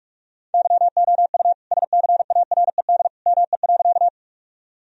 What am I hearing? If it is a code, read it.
Morse code, heard as a keyed tone, about 41 wpm, 700 Hz. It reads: YOU SCARED ME.
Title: Sentences from Top 1000 Words - Review Part 4 - 45wpm